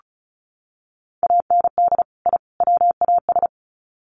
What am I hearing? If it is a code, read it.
Morse code, heard as a keyed tone, about 34 wpm, 700 Hz. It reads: AND I WAS
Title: Sets of 3 Words 35wpm